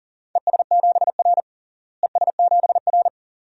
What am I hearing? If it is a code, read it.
Morse code, heard as a keyed tone, about 40 wpm, 700 Hz. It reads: ES7R ES7R
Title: DXCC-100 Call Signs - 4 Characters - Repeated 2x - 40wpm